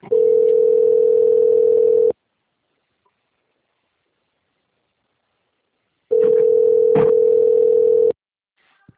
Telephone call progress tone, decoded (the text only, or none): ringback tone